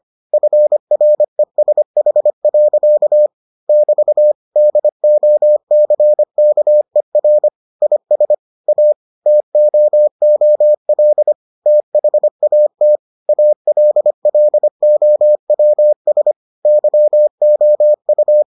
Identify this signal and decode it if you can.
Morse code, heard as a keyed tone, about 25 wpm, 600 Hz. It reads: FRESH. <BT> DOCKER IS A TOOL THAT ALLOWS YOU